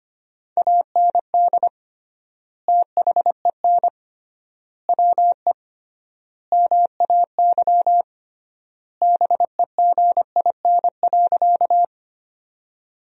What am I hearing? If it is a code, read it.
Morse code, heard as a keyed tone, about 25 wpm, 700 Hz. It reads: AND THEN WE MAY BEGIN.
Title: 3x Word-Spacing - Sentences from Top 300 Words - 25wpm